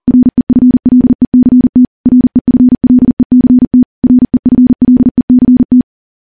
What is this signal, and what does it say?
Morse code, heard as a keyed tone, about 40 wpm, 250 Hz. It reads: REFLECT REFLECT REFLECT
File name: morse / morse-training-40WPM